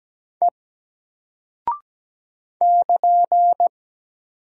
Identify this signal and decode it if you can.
Morse code, heard as a keyed tone, about 17 wpm, 700 Hz. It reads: E <KN>
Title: Single Character (Letter:Number:Punctuation:Prosign) Alt 17wpm